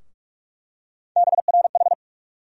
Morse code, heard as a keyed tone, about 45 wpm, 700 Hz. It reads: BRH
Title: USA Call Sign Postfixes - 3 Letters 45wpm